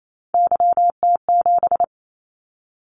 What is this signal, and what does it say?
Morse code, heard as a keyed tone, about 28 wpm, 700 Hz. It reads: YT7
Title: Callsign Prefixes for Top 25 DXCC - 28wpm